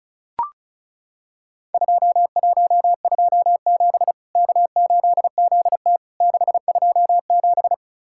Morse code, heard as a keyed tone, about 36 wpm, 700 Hz. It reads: 2127 K8ZT 627
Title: Worked All Europe DX Contest - QTC Exchange - 35wpm